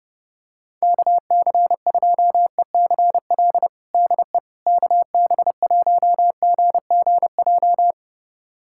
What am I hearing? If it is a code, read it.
Morse code, heard as a keyed tone, about 30 wpm, 700 Hz. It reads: KC2ECL DE KB1GGJ
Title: USA Callsign DE Callsign - 6 Characters - 30wpm